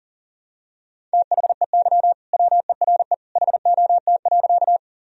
Morse code, heard as a keyed tone, about 40 wpm, 700 Hz. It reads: THEY WERE HOT.
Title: Sentences from Top 1000 Words - Review Part 3 - 40wpm